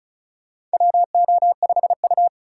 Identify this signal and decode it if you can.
Morse code, heard as a keyed tone, about 36 wpm, 700 Hz. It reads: WO5U
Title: USA Call Signs - 4 Characters - 3x Character Spacing - 35wpm